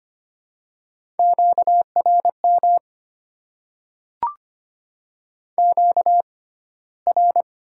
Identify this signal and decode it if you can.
Morse code, heard as a keyed tone, about 25 wpm, 700 Hz. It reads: Q R M Q R
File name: Top 90 QSO Elements - 6x Character Spacing - 25wpm